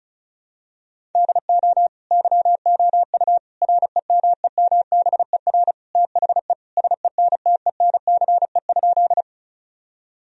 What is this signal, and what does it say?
Morse code, heard as a keyed tone, about 36 wpm, 700 Hz. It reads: DO YOU REMEMBER THE SENTENCE?
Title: Sentences from Top 1000 Words - Review Part 1 - 35wpm